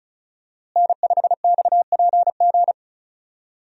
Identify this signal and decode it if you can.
Morse code, heard as a keyed tone, about 36 wpm, 700 Hz. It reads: N5XPG
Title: USA Call Signs - 5 Characters - Repeated 3x - 35wpm